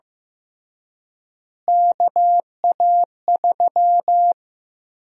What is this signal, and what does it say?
Morse code, heard as a keyed tone, about 15 wpm, 700 Hz. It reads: KA3
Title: USA Call Sign Prefixes - Repeated 2x - 15wpm